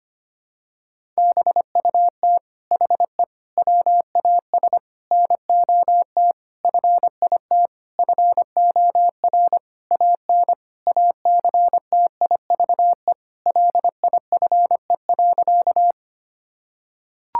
Morse code, heard as a keyed tone, about 25 wpm, 700 Hz. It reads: BUT HE WAS NOT FIT FOR AN ACTIVE LIFE.
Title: Sentences from Top 1000 Words - Review Part 6 - 25wpm